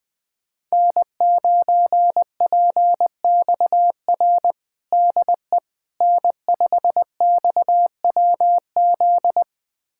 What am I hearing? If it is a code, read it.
Morse code, heard as a keyed tone, about 20 wpm, 700 Hz. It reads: N9PXR DE N5XWZ
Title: USA Callsign DE Callsign - 5 Characters - 20wpm